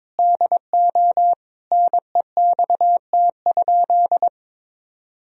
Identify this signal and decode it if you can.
Morse code, heard as a keyed tone, about 22 wpm, 700 Hz. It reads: DO NEXT?
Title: Sentences from Top 1000 Words - Review Part 1 - 22wpm